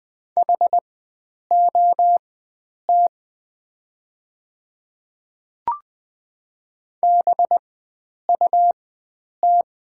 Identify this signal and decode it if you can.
Morse code, heard as a keyed tone, about 20 wpm, spaced as for 9 wpm, 690 Hz. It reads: HOT BUT
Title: 4x Character Spacing - Top 100 Words - 20wpm